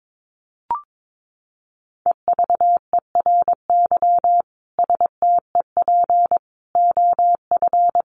Morse code, heard as a keyed tone, about 22 wpm, 700 Hz. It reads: EVERY STEP OF
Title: Sets of 5 Words Encore2 22wpm